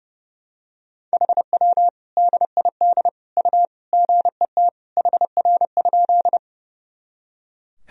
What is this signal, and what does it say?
Morse code, heard as a keyed tone, about 30 wpm, 700 Hz. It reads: HW DID U GET HR?
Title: Sentences from Top 200 Words and Abbreviations 30wpm